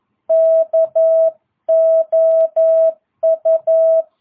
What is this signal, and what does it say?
Morse code, heard as a keyed tone, about 11 wpm, 650 Hz. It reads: KOU